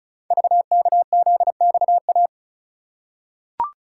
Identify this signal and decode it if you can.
Morse code, heard as a keyed tone, about 36 wpm, 700 Hz. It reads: VKZXA
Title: Code Groups - 5 Letters - 35wpm